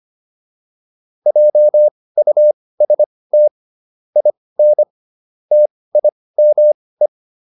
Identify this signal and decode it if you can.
Morse code, heard as a keyed tone, about 25 wpm, spaced as for 18 wpm, 600 Hz. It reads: JUST IN TIME